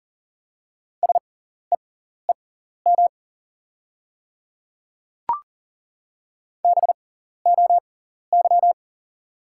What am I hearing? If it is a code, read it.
Morse code, heard as a keyed tone, about 40 wpm, spaced as for 14 wpm, 690 Hz. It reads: SEEM BOY
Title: 6x Character Spacing - Top 300 Words - 40wpm